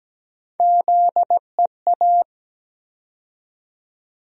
Morse code, heard as a keyed tone, about 17 wpm, 700 Hz. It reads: ZEA
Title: USA Call Sign Postfixes - 3 Letters 17wpm